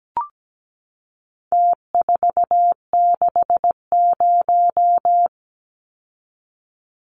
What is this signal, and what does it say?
Morse code, heard as a keyed tone, about 17 wpm, 700 Hz. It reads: T460
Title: Numbers - Contest Style - 4-Digit 17wpm